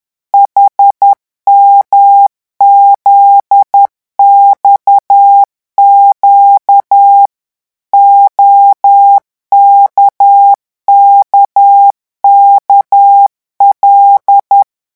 Morse code, heard as a keyed tone, about 10 wpm, 800 Hz. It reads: HMZXQ OKKKL
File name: morse mp3 / revision11 lettres vitesse 10